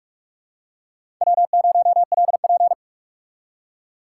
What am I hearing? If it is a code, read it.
Morse code, heard as a keyed tone, about 44 wpm, 700 Hz. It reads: W0LP
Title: USA Call Signs - 6x Character Spacing - 45wpm